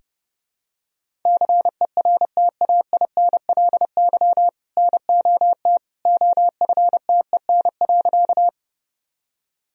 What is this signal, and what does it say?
Morse code, heard as a keyed tone, about 28 wpm, 700 Hz. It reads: CERTAINLY NOT OFTEN.
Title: Sentences from Top 1000 Words - Review Part 1 - 30wpm